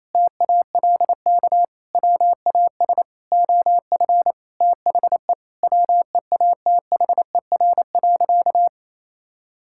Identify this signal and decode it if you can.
Morse code, heard as a keyed tone, about 28 wpm, 700 Hz. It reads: TALK WAS OF THE WEATHER.
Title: Sentences from Top 1000 Words - Review Part 5 - 28wpm